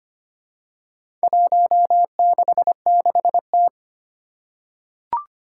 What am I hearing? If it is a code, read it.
Morse code, heard as a keyed tone, about 25 wpm, 700 Hz. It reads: 166T
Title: Four Challenging Letter-Number Combinations 25wpm